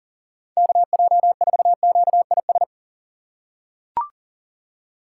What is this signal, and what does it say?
Morse code, heard as a keyed tone, about 37 wpm, 700 Hz. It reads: KJ4QIS
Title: USA Call Signs - 6 Characters - 4x Character Spacing - 40wpm